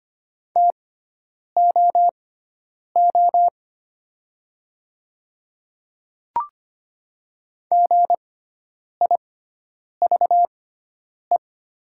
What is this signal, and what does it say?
Morse code, heard as a keyed tone, about 25 wpm, spaced as for 9 wpm, 700 Hz. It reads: TOO GIVE
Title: 6x Character Spacing - Top 200 Words - 25wpm